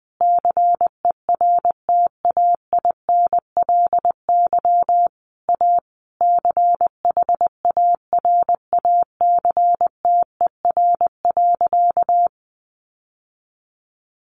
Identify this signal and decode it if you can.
Morse code, heard as a keyed tone, about 20 wpm, 700 Hz. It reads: CERTAINLY A CHARACTER.
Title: Sentences from Top 1200 Words 20wpm